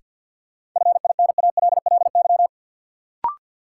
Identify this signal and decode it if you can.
Morse code, heard as a keyed tone, about 50 wpm, 700 Hz. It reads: FINALLY
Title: Sentences from Top 1000 Words - New Word Drill - 50wpm